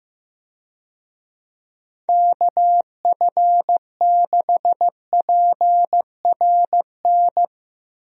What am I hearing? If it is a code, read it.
Morse code, heard as a keyed tone, about 15 wpm, 700 Hz. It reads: KF6PRN
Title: USA Call Signs - 6 Characters - Repeated 2x - 15wpm